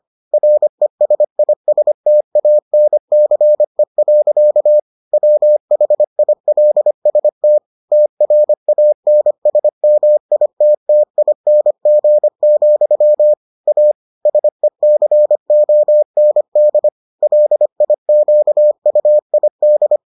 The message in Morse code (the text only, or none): RESISTANCE. WHILST TRANSMITTING, A SECOND LIQUID